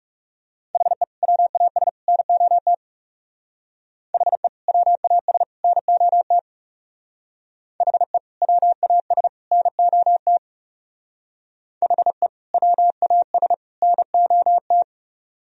Morse code, heard as a keyed tone, about 37 wpm, 700 Hz. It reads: HE WAS NOT HE WAS NOT HE WAS NOT HE WAS NOT
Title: Speed-Racing - Sets of 3 Words 30wpm